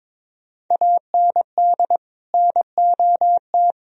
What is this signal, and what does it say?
Morse code, heard as a keyed tone, about 21 wpm, 700 Hz. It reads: AND NOT
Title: Sentences from Top 1000 Words - Review Part 3 - 22wpm